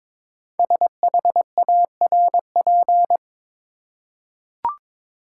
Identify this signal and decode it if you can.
Morse code, heard as a keyed tone, about 23 wpm, 700 Hz. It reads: SHARP
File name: 5 Letter Words - First 3 Letters Given - 22wpm